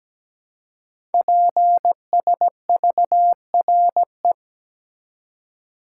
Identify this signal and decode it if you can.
Morse code, heard as a keyed tone, about 17 wpm, 700 Hz. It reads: PSVRE
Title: Code Groups - 5 Letter:Numbers - 17wpm